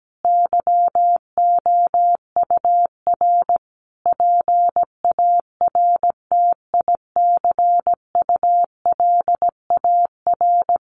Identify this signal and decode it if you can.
Morse code, heard as a keyed tone, about 17 wpm, 700 Hz. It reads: YOUR PARTICULAR
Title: Sentences from Top 800 Words 17wpm